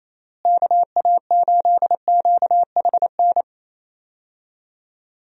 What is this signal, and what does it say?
Morse code, heard as a keyed tone, about 28 wpm, 700 Hz. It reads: KA8QHN
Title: USA Call Signs - 6 Characters - Repeated 3x - 28wpm